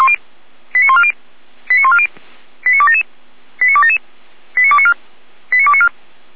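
Five-tone selective call (ZVEI), heard as quick runs of five tones, tones of about 70 ms. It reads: 89140 89140 89277 89277 89294 89294
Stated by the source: Five-tone ZVEI selective call received on SDR, from Feuerwehr Orlamünde fire brigade audio server.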